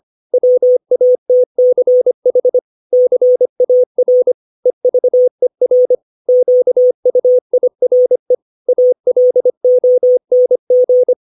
Morse code, heard as a keyed tone, about 25 wpm, 500 Hz. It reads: WATCH CAR EVER QUIRE ALONG
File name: 632 words / 632 Words 25WPM